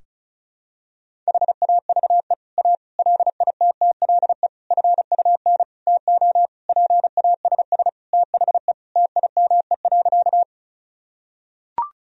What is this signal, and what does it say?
Morse code, heard as a keyed tone, about 36 wpm, 700 Hz. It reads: HAVE A LITTLE FUN TO PASS THE TIME.
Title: Sentences from Top 1000 Words - Review Part 4 - 35wpm